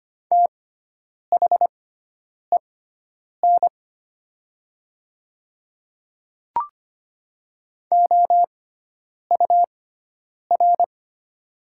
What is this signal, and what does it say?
Morse code, heard as a keyed tone, about 25 wpm, spaced as for 9 wpm, 700 Hz. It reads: THEN OUR
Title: 6x Character Spacing - Top 300 Words - 25wpm